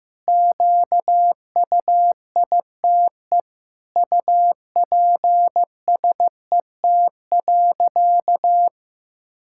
Morse code, heard as a keyed tone, about 15 wpm, 700 Hz. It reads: QUITE UPSET.
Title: Sentences from Top 1000 Words - Review Part 5 - 15wpm